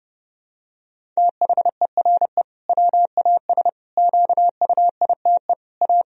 Morse code, heard as a keyed tone, about 30 wpm, 700 Hz. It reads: THERE WAS QUITE A
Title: Sentences from Top 1000 Words - Review Part 1 - 30wpm